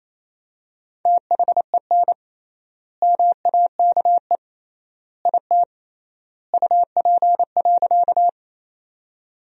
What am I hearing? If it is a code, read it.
Morse code, heard as a keyed tone, about 28 wpm, 700 Hz. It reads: THEN MAKE IT UP.
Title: 3x Word-Spacing - Sentences from Top 100 Words - 28wpm